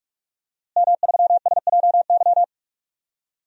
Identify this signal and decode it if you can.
Morse code, heard as a keyed tone, about 43 wpm, 700 Hz. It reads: M3SJY
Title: DXCC-100 Call Signs - 5 Characters - Repeated 3x - 45wpm